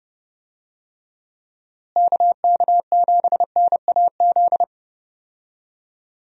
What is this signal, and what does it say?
Morse code, heard as a keyed tone, about 30 wpm, 700 Hz. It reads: KK7NAZ
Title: USA Call Signs - 6 Characters - Repeated 3x - 30wpm